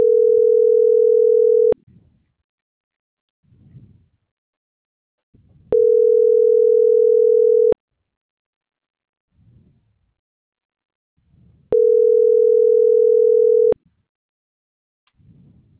Ringback tone (North America), 440 + 480 Hz.
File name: CENTRAL MAERSK / out gabrielle.melendez - .28540